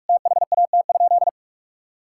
Morse code, heard as a keyed tone, about 46 wpm, 700 Hz. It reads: THAT?
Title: Sentences from Top 1000 Words - Review Part 5 - 45wpm